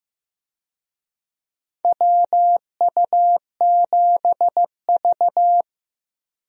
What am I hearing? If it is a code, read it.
Morse code, heard as a keyed tone, about 15 wpm, 700 Hz. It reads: WU7V
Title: USA Call Signs - 4 Characters - 15wpm